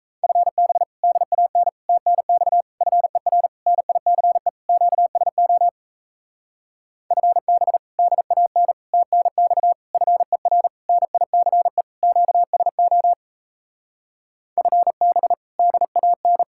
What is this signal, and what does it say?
Morse code, heard as a keyed tone, about 38 wpm, 700 Hz. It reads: FB DAN TNX FER NICE QSO FB DAN TNX FER NICE QSO FB DAN